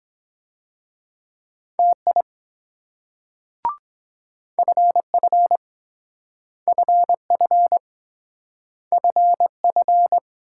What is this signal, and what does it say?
Morse code, heard as a keyed tone, about 23 wpm, 700 Hz. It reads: TI FF FF FF